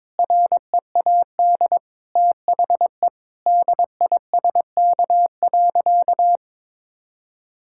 Morse code, heard as a keyed tone, about 22 wpm, 700 Hz. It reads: READ THE DISK.
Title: Sentences from Top 1000 Words - Review Part 6 - 22wpm